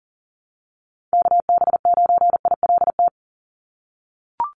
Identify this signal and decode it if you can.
Morse code, heard as a keyed tone, about 40 wpm, 700 Hz. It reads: KB9ILT
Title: NA SOTA Activator Callsigns 40wpm